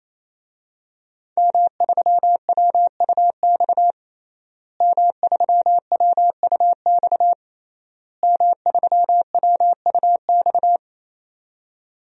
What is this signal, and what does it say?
Morse code, heard as a keyed tone, about 28 wpm, 700 Hz. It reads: M3WUX M3WUX M3WUX
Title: DXCC-100 Call Signs - 5 Characters - Repeated 3x - 28wpm